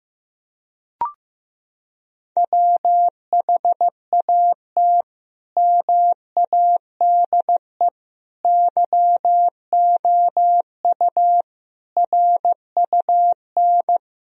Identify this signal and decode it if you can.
Morse code, heard as a keyed tone, about 15 wpm, 700 Hz. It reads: WHAT MADE YOU RUN
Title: Sentences from Top 1000 Words - Review Part 4 - 15wpm